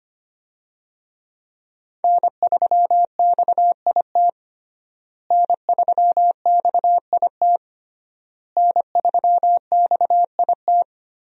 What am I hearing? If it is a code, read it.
Morse code, heard as a keyed tone, about 25 wpm, 700 Hz. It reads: N3XIT N3XIT N3XIT
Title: USA Call Signs - 5 Characters - Repeated 3x - 25wpm